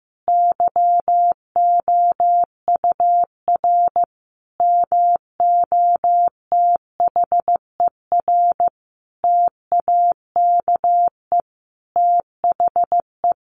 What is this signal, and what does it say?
Morse code, heard as a keyed tone, about 15 wpm, 700 Hz. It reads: YOUR MOTHER TAKE THE
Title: Sentences from Top 700 Words 15wpm